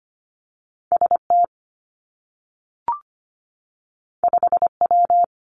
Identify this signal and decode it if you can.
Morse code, heard as a keyed tone, about 25 wpm, 700 Hz. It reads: S T 5 W
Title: Two Challenging Letter-Number Combinations 25wpm